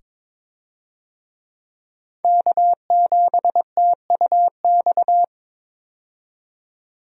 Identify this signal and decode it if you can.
Morse code, heard as a keyed tone, about 22 wpm, 700 Hz. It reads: K7TUX
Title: USA Call Signs - 5 Characters - 22wpm